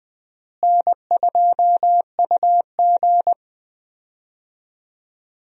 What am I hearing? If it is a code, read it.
Morse code, heard as a keyed tone, about 20 wpm, 700 Hz. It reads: N2UG
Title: USA Call Signs - 4 Characters - Repeated 3x - 20wpm